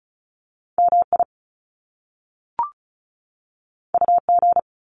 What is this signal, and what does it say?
Morse code, heard as a keyed tone, about 36 wpm, 700 Hz. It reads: M I U G
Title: Top Two Letter Combinations in Words 35wpm